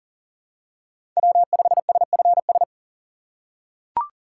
Morse code, heard as a keyed tone, about 40 wpm, 700 Hz. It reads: W5SFS
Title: USA Call Signs - 5 Characters - 4x Character Spacing - 40wpm